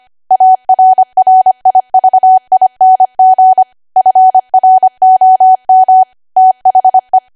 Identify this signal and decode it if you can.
Morse code, heard as a keyed tone, about 25 wpm, 750 Hz. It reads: ARRIVING FROM THE
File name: Archive / 25 WPM / 25WPM